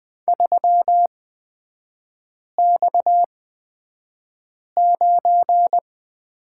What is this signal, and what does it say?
Morse code, heard as a keyed tone, about 20 wpm, spaced as for 5 wpm, 700 Hz. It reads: 3X9
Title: 229 Lesson 37e - Letter x - Three Characters 20wpm